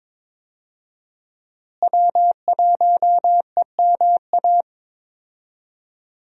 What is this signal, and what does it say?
Morse code, heard as a keyed tone, about 21 wpm, 700 Hz. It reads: W1EMA
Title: USA Call Signs - 5 Characters - Repeated 3x - 22wpm